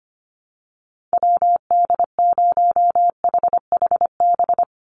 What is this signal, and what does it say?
Morse code, heard as a keyed tone, about 25 wpm, 700 Hz. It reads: WD0HHB
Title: USA Call Signs 25wpm